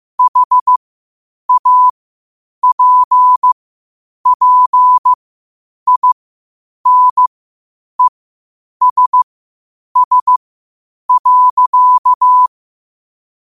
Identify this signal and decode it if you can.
Morse code, heard as a keyed tone, about 15 wpm, spaced as for 8 wpm, 1000 Hz. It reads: HAPPINESS.